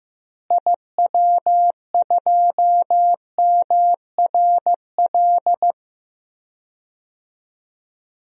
Morse code, heard as a keyed tone, about 15 wpm, 700 Hz. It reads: IW2MRL
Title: DXCC-100 Call Signs - 6 Characters - Repeated 3x - 15wpm